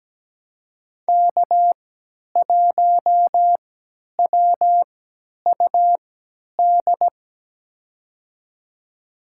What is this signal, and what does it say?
Morse code, heard as a keyed tone, about 17 wpm, spaced as for 10 wpm, 700 Hz. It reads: K1WUD
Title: USA Call Signs - 5 Characters - 3x Character Spacing - 17wpm